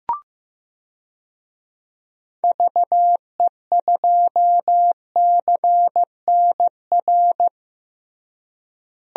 Morse code, heard as a keyed tone, about 15 wpm, 700 Hz. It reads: VE2CNR
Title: DXCC-100 Call Signs - 6 Characters - Repeated 3x - 15wpm